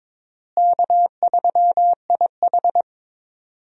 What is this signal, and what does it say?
Morse code, heard as a keyed tone, about 22 wpm, 700 Hz. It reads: K3IH
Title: USA Call Signs - 4 Characters - Repeated 3x - 22wpm